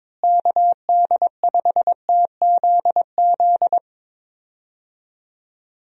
Morse code, heard as a keyed tone, about 22 wpm, 700 Hz. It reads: KD5TZZ
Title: USA Call Signs - 6 Characters - Repeated 2x - 22wpm